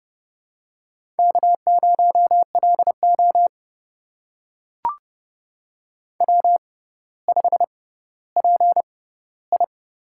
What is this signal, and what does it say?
Morse code, heard as a keyed tone, about 30 wpm, spaced as for 15 wpm, 700 Hz. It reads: K0LO W5PI
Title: USA Call Signs - 4 Characters - 6x Character Spacing - 30wpm